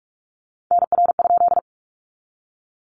Morse code, heard as a keyed tone, about 44 wpm, 700 Hz. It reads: NR?
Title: Common QSO Phrases 45wpm